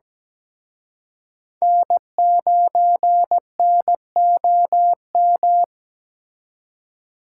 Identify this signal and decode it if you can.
Morse code, heard as a keyed tone, about 20 wpm, 700 Hz. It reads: N9NOM
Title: Speed Racing - USA 1x3 Call Signs 17wpm